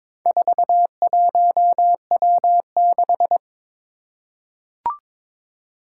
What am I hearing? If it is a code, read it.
Morse code, heard as a keyed tone, about 22 wpm, 700 Hz. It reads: 41W6
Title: Four Challenging Letter-Number Combinations 22wpm